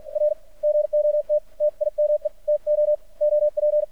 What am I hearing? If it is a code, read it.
Morse code, heard as a keyed tone, about 53 wpm, spaced as for 46 wpm, 600 Hz, with radio noise. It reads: R GOT TIME TO OP
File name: archived / CRSnet